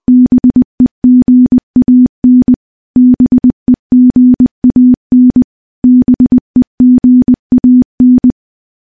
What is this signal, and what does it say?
Morse code, heard as a keyed tone, about 21 wpm, 250 Hz. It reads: BEGAN BEGAN BEGAN
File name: morse-training-20WPM